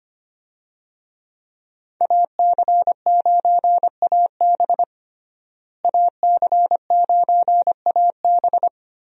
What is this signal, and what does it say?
Morse code, heard as a keyed tone, about 25 wpm, 690 Hz. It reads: AC9AB AC9AB
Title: USA Call Signs - 5 Characters - Repeated 2x - 25wpm